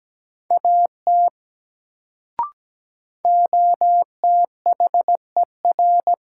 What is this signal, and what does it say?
Morse code, heard as a keyed tone, about 17 wpm, 700 Hz. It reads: AT OTHER
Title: Sentences from Top 100 Words - New Word Drill - 17wpm